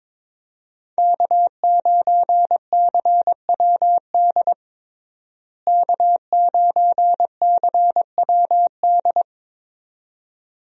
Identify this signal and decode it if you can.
Morse code, heard as a keyed tone, about 22 wpm, 700 Hz. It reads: K9CWD K9CWD
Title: USA Call Signs - 5 Characters - Repeated 2x - 22wpm